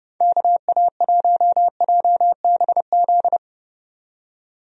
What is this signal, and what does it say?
Morse code, heard as a keyed tone, about 30 wpm, 700 Hz. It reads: KA1JBZ
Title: USA Call Signs - 6 Characters - 30wpm